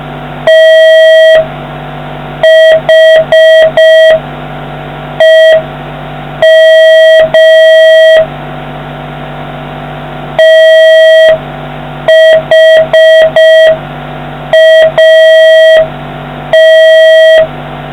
Morse code, heard as a keyed tone, about 5 wpm, 1950 Hz, with noise. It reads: THEM THAT